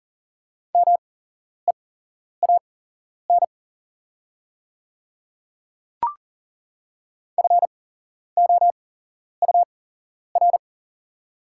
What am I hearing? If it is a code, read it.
Morse code, heard as a keyed tone, about 42 wpm, spaced as for 11 wpm, 700 Hz. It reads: MEAN FOUR
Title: 8x Character Spacing - Top 300 Words - 40wpm